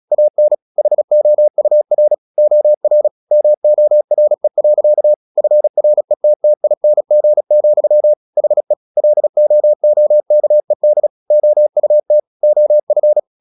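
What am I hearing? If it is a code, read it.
Morse code, heard as a keyed tone, about 36 wpm, 600 Hz. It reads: AN HOUR OR MORE. FRETTING, HE LOOKED OUT OF
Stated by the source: Morse practice at 35 wpm from CW med Gnister, sending Ken Follett's novel Never in English.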